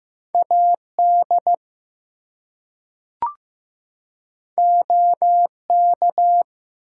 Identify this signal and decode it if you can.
Morse code, heard as a keyed tone, about 15 wpm, 700 Hz. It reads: AD OK
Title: Two Letter Words - 15wpm